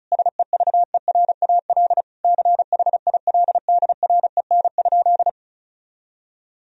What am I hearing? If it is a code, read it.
Morse code, heard as a keyed tone, about 36 wpm, 700 Hz. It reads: SEVERAL CHILDREN?
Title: Sentences from Top 1000 Words - Review Part 2 - 35wpm